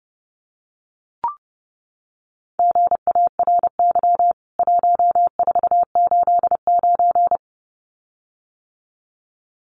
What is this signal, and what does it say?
Morse code, heard as a keyed tone, about 30 wpm, 700 Hz. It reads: GARY 1489
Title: CWT Exchange 30wpm